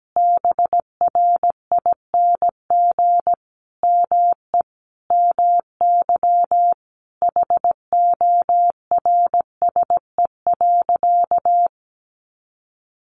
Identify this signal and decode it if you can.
Morse code, heard as a keyed tone, about 17 wpm, 700 Hz. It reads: BRING ME MY HORSE.
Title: Sentences from Top 500 Words 17wpm